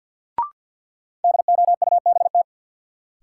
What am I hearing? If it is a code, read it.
Morse code, heard as a keyed tone, about 49 wpm, 700 Hz. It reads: DOUBT
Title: Sentences from Top 1500 Words - New Word Drill - 50wpm